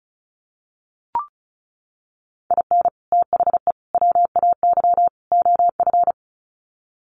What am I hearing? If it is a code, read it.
Morse code, heard as a keyed tone, about 34 wpm, 700 Hz. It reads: IN THE WAY OF
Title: Sets of 4 Words 35wpm